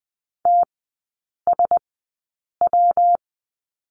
Morse code, heard as a keyed tone, about 20 wpm, spaced as for 8 wpm, 700 Hz. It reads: TSW
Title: 114 Lesson 19e - Letter w - Three Characters 20wpm